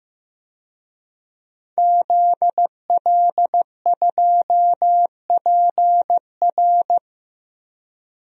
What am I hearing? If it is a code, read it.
Morse code, heard as a keyed tone, about 15 wpm, 700 Hz. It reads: ZL2PR
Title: DXCC-100 Call Signs - 5 Characters - Repeated 3x - 15wpm